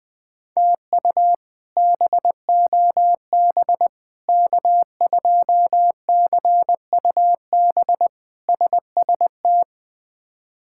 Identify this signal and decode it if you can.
Morse code, heard as a keyed tone, about 20 wpm, 700 Hz. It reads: TU BOB K2CUB SST
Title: K1USN SST QSO Practice - 20wpm